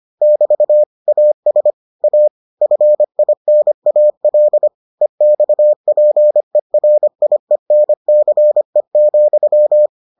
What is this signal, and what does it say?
Morse code, heard as a keyed tone, about 25 wpm, 600 Hz. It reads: <BT> AS A FINAL EXPERIENCE,